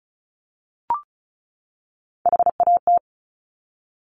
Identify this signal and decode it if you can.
Morse code, heard as a keyed tone, about 36 wpm, 700 Hz. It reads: HAT
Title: Top Three Letter Combinations in Words 35wpm